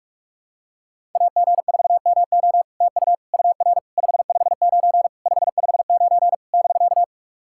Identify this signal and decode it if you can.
Morse code, heard as a keyed tone, about 44 wpm, 690 Hz. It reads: AG4MO TU UR 559 559 <BK>
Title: SOTA QSO Practice 45wpm